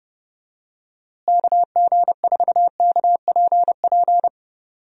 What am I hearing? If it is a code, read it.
Morse code, heard as a keyed tone, about 30 wpm, 700 Hz. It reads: KG4KPP
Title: USA Call Signs - 6 Characters - Repeated 3x - 30wpm